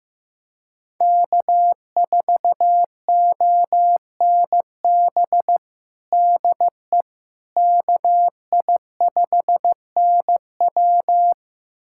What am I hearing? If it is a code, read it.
Morse code, heard as a keyed tone, about 16 wpm, 700 Hz. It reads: K4ONB DE KI5NW
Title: USA Callsign DE Callsign 15wpm